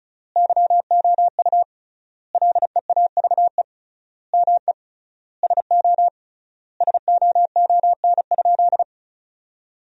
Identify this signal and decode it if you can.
Morse code, heard as a keyed tone, about 36 wpm, 700 Hz. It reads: YOU LEAVE ME SO SOON?
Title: 3x Word-Spacing - Sentences from Top 400 Words - 35wpm